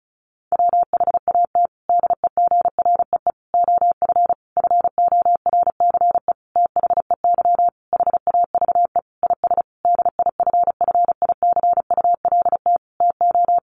Morse code, heard as a keyed tone, about 34 wpm, 700 Hz. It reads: WHAT DEGREE OF FORCE THEY HAVE IS DIFFICULT TO
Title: Sentences from Top 1500 Words 35wpm